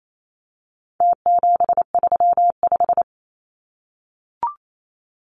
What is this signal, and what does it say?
Morse code, heard as a keyed tone, about 28 wpm, 700 Hz. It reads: T735
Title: Numbers - Contest Style - 4-Digit 28wpm